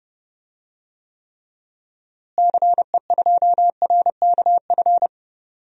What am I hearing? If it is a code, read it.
Morse code, heard as a keyed tone, about 30 wpm, 700 Hz. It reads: CE2RKF
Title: DXCC-100 Call Signs - 6 Characters - Repeated 2x - 30wpm